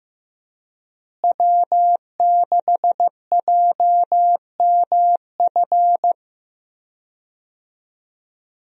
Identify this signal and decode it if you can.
Morse code, heard as a keyed tone, about 15 wpm, 700 Hz. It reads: W6JMF
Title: USA Call Signs - 5 Characters - Repeated 3x - 15wpm